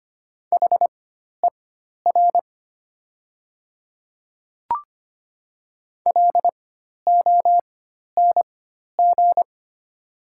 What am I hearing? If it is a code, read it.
Morse code, heard as a keyed tone, about 25 wpm, spaced as for 12 wpm, 700 Hz. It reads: HER LONG
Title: 4x Character Spacing - Top 200 Words - 25wpm